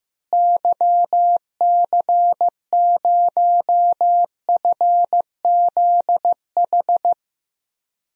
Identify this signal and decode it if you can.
Morse code, heard as a keyed tone, about 16 wpm, 700 Hz. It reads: YC0FZH
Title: DXCC-100 Call Signs - 6 Characters - Repeated 2x - 15wpm